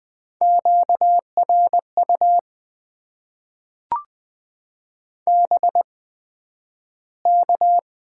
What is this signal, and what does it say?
Morse code, heard as a keyed tone, about 20 wpm, spaced as for 11 wpm, 700 Hz. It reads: QRU BK
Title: Top 90 QSO Elements - 8x Character Spacing - 20wpm